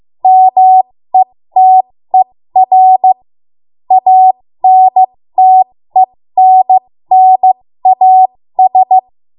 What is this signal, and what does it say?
Morse code, heard as a keyed tone, about 15 wpm, spaced as for 13 wpm, 750 Hz. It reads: METER ANTENNAS
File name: Morse / 13WPM